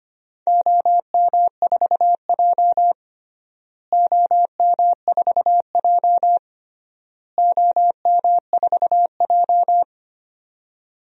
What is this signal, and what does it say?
Morse code, heard as a keyed tone, about 25 wpm, 700 Hz. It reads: OM4J OM4J OM4J
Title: DXCC-100 Call Signs - 4 Characters - Repeated 3x - 25wpm